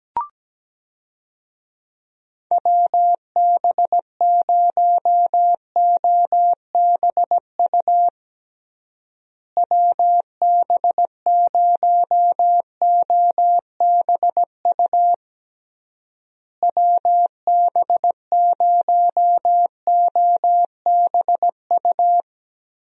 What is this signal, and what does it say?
Morse code, heard as a keyed tone, about 17 wpm, 700 Hz. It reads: WB0OBU WB0OBU WB0OBU
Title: USA Call Signs - 6 Characters - Repeated 3x - 17wpm